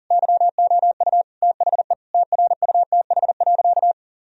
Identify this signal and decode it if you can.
Morse code, heard as a keyed tone, about 40 wpm, 700 Hz. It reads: YOU THE TRUTH.
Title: Sentences from Top 1000 Words - Review Part 2 - 40wpm